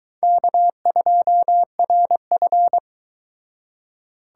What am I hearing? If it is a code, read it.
Morse code, heard as a keyed tone, about 23 wpm, 700 Hz. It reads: K2RF